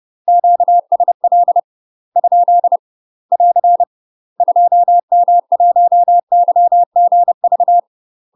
Morse code, heard as a keyed tone, about 30 wpm, 700 Hz. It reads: QSL ? <AR> 2M1YGV
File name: UK MASTER QSOs TRAINING / 30WPM / UK MASTER QSOs 0007